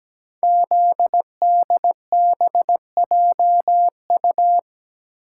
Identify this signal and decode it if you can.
Morse code, heard as a keyed tone, about 17 wpm, 700 Hz. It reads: ZDBJU
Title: Code Groups - 5 Letters - 17wpm